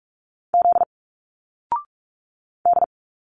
Morse code, heard as a keyed tone, about 46 wpm, 700 Hz. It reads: Z D
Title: Single Letters 45wpm